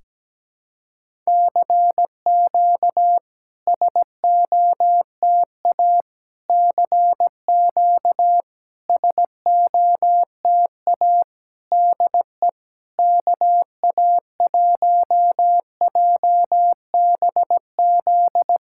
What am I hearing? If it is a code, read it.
Morse code, heard as a keyed tone, about 17 wpm, 690 Hz. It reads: CQ SOTA CQ SOTA DE KA1JBZ
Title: SOTA QSO Practice 17wpm